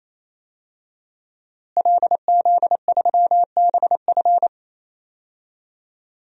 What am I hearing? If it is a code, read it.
Morse code, heard as a keyed tone, about 28 wpm, 700 Hz. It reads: LZ3BF
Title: DXCC-100 Call Signs - 5 Characters - Repeated 3x - 28wpm